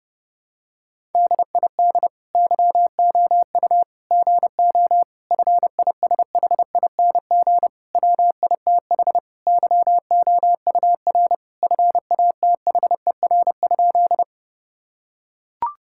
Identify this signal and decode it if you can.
Morse code, heard as a keyed tone, about 30 wpm, 700 Hz. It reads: DID YOU GO FISHING WITH YOUR FATHER?
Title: Sentences from Top 1000 Words - Review Part 5 - 30wpm